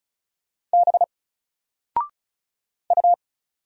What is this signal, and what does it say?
Morse code, heard as a keyed tone, about 33 wpm, 700 Hz. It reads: B U
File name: Single-Letter-Number-35wpm